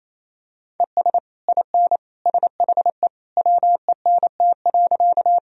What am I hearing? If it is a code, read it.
Morse code, heard as a keyed tone, about 28 wpm, 700 Hz. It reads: ES IN SHE WENT.
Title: Sentences from Top 200 Words and Abbreviations 28wpm